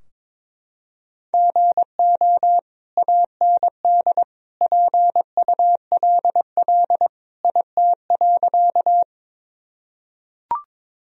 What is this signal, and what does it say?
Morse code, heard as a keyed tone, about 22 wpm, 700 Hz. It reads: GO AND PULL IT.
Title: Sentences from Top 1000 Words - Review Part 2 - 22wpm